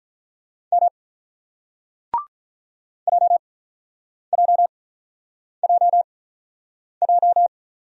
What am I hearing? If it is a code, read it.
Morse code, heard as a keyed tone, about 48 wpm, spaced as for 9 wpm, 700 Hz. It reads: M JJJJ